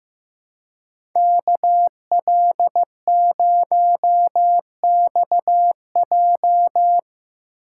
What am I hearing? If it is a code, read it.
Morse code, heard as a keyed tone, about 17 wpm, 700 Hz. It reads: KL0XJ
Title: Speed Racing - USA 2x2 Call Signs 15wpm